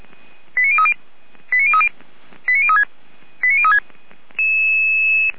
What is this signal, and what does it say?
Five-tone selective call (ZVEI), heard as quick runs of five tones, tones of about 70 ms. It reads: 89030 89030 89036 89036